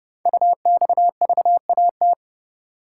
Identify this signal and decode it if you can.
Morse code, heard as a keyed tone, about 30 wpm, 700 Hz. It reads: UXVAT
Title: Code Groups - 5 Letters - 30wpm